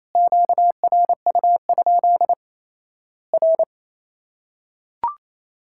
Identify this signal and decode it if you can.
Morse code, heard as a keyed tone, about 28 wpm, 700 Hz. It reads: QRU? R
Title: Worked All Europe DX Contest - QTC Exchange - 28wpm